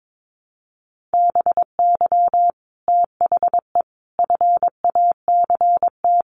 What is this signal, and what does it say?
Morse code, heard as a keyed tone, about 22 wpm, 700 Hz. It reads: BY THE FACT
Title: Sets of 5 Words Encore 22wpm